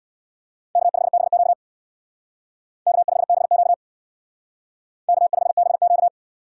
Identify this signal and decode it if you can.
Morse code, heard as a keyed tone, about 62 wpm, 700 Hz. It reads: K5BQ K5BQ K5BQ
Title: Speed Racing - USA Call Signs 50wpm